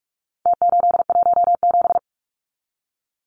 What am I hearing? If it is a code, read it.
Morse code, heard as a keyed tone, about 44 wpm, 700 Hz. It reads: T817
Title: Numbers - Contest Style - 4-Digit 45wpm